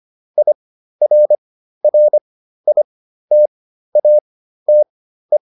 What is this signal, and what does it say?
Morse code, heard as a keyed tone, about 25 wpm, spaced as for 13 wpm, 600 Hz. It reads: IRRITATE